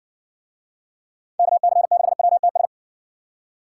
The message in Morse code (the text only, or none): KC6WTS